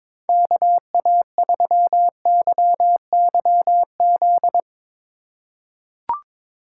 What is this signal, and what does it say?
Morse code, heard as a keyed tone, about 22 wpm, 700 Hz. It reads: KA3YYZ
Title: USA Call Signs - 3x Character Spacing - 22wpm